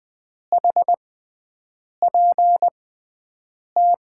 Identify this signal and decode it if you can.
Morse code, heard as a keyed tone, about 21 wpm, spaced as for 7 wpm, 700 Hz. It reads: HPT
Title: USA Call Signs - 6 Characters - 6x Character Spacing - 20wpm